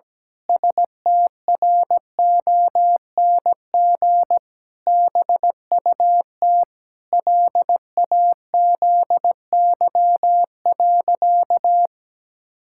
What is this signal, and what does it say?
Morse code, heard as a keyed tone, about 17 wpm, 700 Hz. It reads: STRONG BUT LAZY.
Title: Pangram Challenge v2 - 17wpm